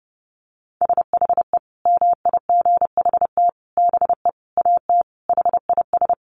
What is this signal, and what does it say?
Morse code, heard as a keyed tone, about 30 wpm, 700 Hz. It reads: SHE MIGHT BE AT HIS
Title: Sentences from Top 300 Words 30wpm